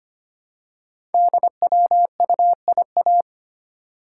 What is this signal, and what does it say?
Morse code, heard as a keyed tone, about 25 wpm, 690 Hz. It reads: DWUIA
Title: Code Groups - 5 Letter:Numbers - 25wpm